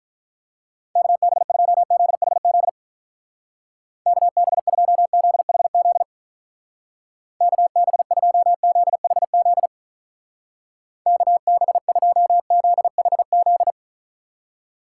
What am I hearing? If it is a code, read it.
Morse code, heard as a keyed tone, about 42 wpm, 700 Hz. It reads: KB2ZHZ KB2ZHZ KB2ZHZ KB2ZHZ
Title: Speed Racing - USA Call Signs 35wpm